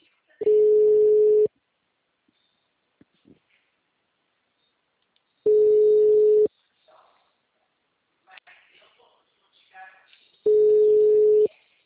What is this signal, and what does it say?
Ringback tone (Europe, 425 Hz), with speech.